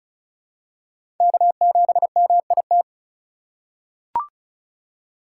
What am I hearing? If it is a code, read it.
Morse code, heard as a keyed tone, about 34 wpm, 700 Hz. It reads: K7MIT
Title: USA Call Signs - 5 Characters - 3x Character Spacing - 35wpm